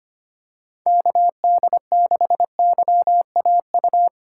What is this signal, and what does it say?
Morse code, heard as a keyed tone, about 25 wpm, 700 Hz. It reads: KD6YAU
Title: USA Call Signs - 6 Characters - 2x Character Spacing - 25wpm